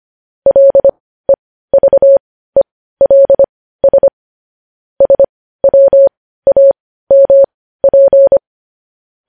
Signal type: Morse code